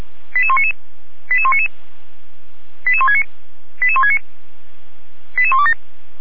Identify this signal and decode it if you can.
Five-tone selective call (ZVEI), heard as quick runs of five tones, tones of about 70 ms. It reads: 88199 88199 88169 88169 88127
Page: SDR/ZVEI